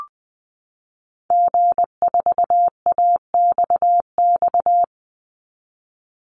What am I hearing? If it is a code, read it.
Morse code, heard as a keyed tone, about 20 wpm, 700 Hz. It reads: G4AXX
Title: Call Signs - 20wpm